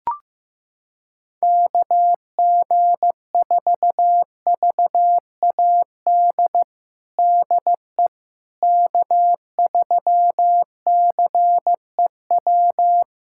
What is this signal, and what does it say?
Morse code, heard as a keyed tone, about 15 wpm, 700 Hz. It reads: KG4VAD DE K3CEW
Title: USA Callsign DE Callsign 15wpm